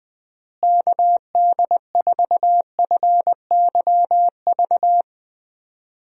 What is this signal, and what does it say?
Morse code, heard as a keyed tone, about 21 wpm, 700 Hz. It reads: KD4FYV
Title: USA Call Signs - 6 Characters - Repeated 3x - 20wpm